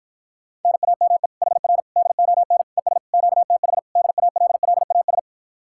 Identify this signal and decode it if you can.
Morse code, heard as a keyed tone, about 53 wpm, 700 Hz. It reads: NAME HR DON ES QTH DALLAS